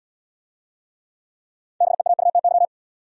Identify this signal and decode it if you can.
Morse code, heard as a keyed tone, about 74 wpm, 700 Hz. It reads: BEAUTY